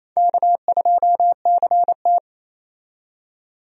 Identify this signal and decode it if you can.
Morse code, heard as a keyed tone, about 28 wpm, 700 Hz. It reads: K2CT
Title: USA Call Signs - 4 Characters - 28wpm